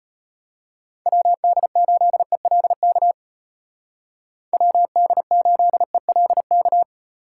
Morse code, heard as a keyed tone, about 32 wpm, 700 Hz. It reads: WD8ELK WD8ELK